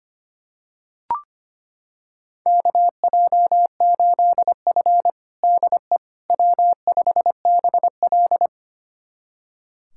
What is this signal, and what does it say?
Morse code, heard as a keyed tone, about 25 wpm, 700 Hz. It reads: KJ8F DE W5BL
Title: USA Callsign DE Callsign - 4 Characters - 25wpm